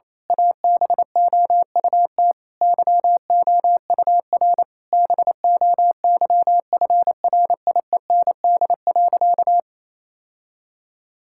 Morse code, heard as a keyed tone, about 28 wpm, 700 Hz. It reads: ABOUT YOUR BOYFRIEND.
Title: Sentences from Top 1000 Words - Review Part 5 - 28wpm